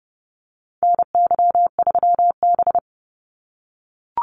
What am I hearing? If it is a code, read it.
Morse code, heard as a keyed tone, about 30 wpm, 700 Hz. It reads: NY3B
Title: Call Signs - 30wpm